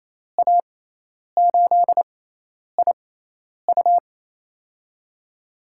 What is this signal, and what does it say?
Morse code, heard as a keyed tone, about 28 wpm, spaced as for 10 wpm, 700 Hz. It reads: A8IU
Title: USA Call Signs - 5 Characters - 6x Character Spacing - 28wpm